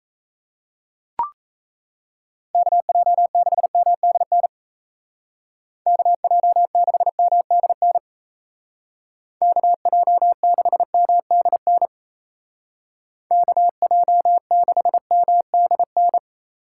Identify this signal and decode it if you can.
Morse code, heard as a keyed tone, about 35 wpm, 700 Hz. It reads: KJ6MDN KJ6MDN KJ6MDN KJ6MDN
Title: Speed Racing - USA Call Signs 28wpm